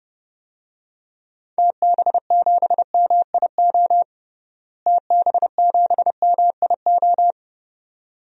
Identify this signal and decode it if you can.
Morse code, heard as a keyed tone, about 30 wpm, 700 Hz. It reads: TB7MIO TB7MIO
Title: DXCC-100 Call Signs - 6 Characters - Repeated 2x - 30wpm